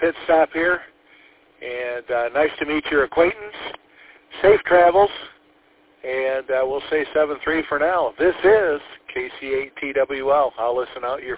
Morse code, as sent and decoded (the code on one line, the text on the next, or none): none